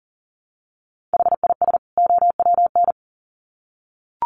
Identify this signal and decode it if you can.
Morse code, heard as a keyed tone, about 40 wpm, 700 Hz. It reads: HIS OWN
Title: Sets of 2 Words 40wpm